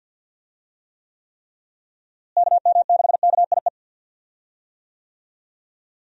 Morse code, heard as a keyed tone, about 51 wpm, 700 Hz. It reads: KM6KIE
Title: USA Call Signs - 6 Characters - Repeated 3x - 50wpm